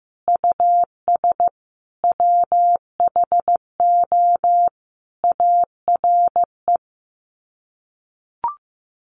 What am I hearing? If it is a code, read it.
Morse code, heard as a keyed tone, about 15 wpm, 700 Hz. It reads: US WHO ARE
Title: Sets of 5 Words Encore2 15wpm